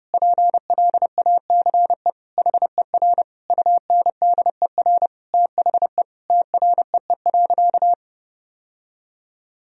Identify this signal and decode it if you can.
Morse code, heard as a keyed tone, about 32 wpm, 700 Hz. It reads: PLACE HER UNDER THE TREE.
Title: Sentences from Top 1000 Words - Review Part 1 - 30wpm